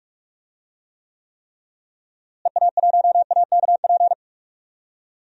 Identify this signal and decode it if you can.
Morse code, heard as a keyed tone, about 46 wpm, 700 Hz. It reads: EA1AKP
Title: DXCC-100 Call Signs - 6 Characters - Repeated 3x - 45wpm